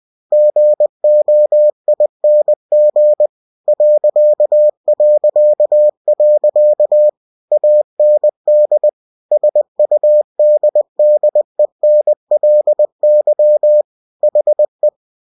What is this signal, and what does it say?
Morse code, heard as a keyed tone, about 20 wpm, 600 Hz. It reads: GOING ... AND SUDDENLY HE